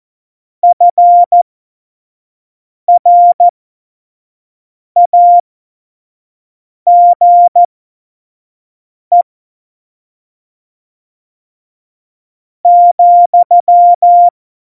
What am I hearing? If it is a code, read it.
Morse code, heard as a keyed tone, about 14 wpm, spaced as for 5 wpm, 700 Hz. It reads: FRAGE ,
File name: Morsekurs / Witze / Witz 07 Juli 2020, 25 F70